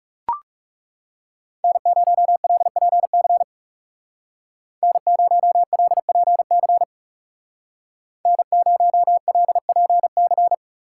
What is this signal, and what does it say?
Morse code, heard as a keyed tone, about 34 wpm, 700 Hz. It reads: N0LPC N0LPC N0LPC
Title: Speed Racing - USA 1x3 Call Signs 30wpm